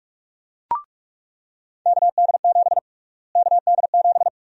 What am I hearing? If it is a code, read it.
Morse code, heard as a keyed tone, about 42 wpm, 700 Hz. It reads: KD7 KD7
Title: USA Call Sign Prefixes - Repeated 2x - 45wpm